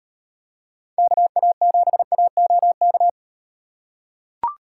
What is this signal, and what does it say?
Morse code, heard as a keyed tone, about 35 wpm, 700 Hz. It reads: KA7AOK